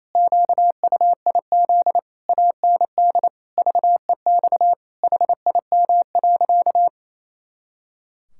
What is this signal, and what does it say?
Morse code, heard as a keyed tone, about 28 wpm, 700 Hz. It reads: QUIZ AND VEX HIM.
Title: Pangram Challenge v2 - 28wpm